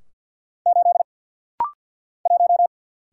Morse code, heard as a keyed tone, about 49 wpm, 700 Hz. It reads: 8 1
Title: Single Letter-Number - Rapid-Fire 50wpm